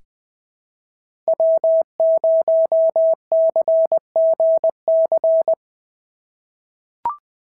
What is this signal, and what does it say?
Morse code, heard as a keyed tone, about 19 wpm, 650 Hz. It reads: W0CGC
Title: POTA QSO Practice - 20wpm